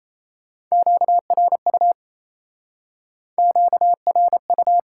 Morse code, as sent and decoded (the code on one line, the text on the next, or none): --.- .-. ..- / --.- .-. ..-
QRU QRU